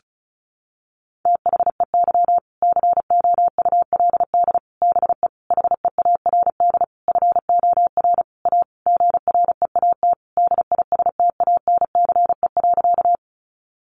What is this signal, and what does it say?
Morse code, heard as a keyed tone, about 34 wpm, 700 Hz. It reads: THEY COULD BE HEARD FOR A GREAT DISTANCE.
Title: Sentences from Top 1200 Words 35wpm